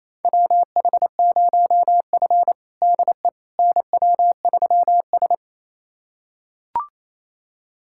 Morse code, heard as a keyed tone, about 28 wpm, 700 Hz. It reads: WH0F DE NW3S
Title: USA Callsign DE Callsign - 4 Characters - 28wpm